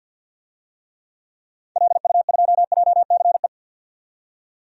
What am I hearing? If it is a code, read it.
Morse code, heard as a keyed tone, about 49 wpm, 700 Hz. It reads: LU2JCE
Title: DXCC-100 Call Signs - 6 Characters - 50wpm